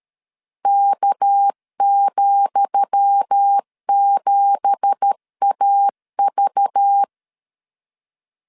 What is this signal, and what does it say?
Morse code, heard as a keyed tone, about 13 wpm, 800 Hz. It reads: K,7AV
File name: LEZIONI / 10 SIGNS / 10D COMA 12wpm 2544